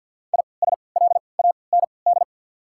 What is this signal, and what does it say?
Morse code, heard as a keyed tone, about 53 wpm, spaced as for 29 wpm, 690 Hz. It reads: ISLAND
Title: 3x Character Spacing - Top 500 Words - 50wpm